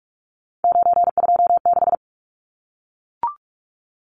Morse code, heard as a keyed tone, about 44 wpm, 700 Hz. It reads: 926
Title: Numbers - 3-Digit 45wpm